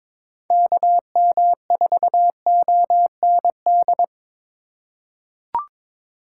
Morse code, heard as a keyed tone, about 22 wpm, 700 Hz. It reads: KM4OND
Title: USA Call Signs - 6 Characters - 4x Character Spacing - 22wpm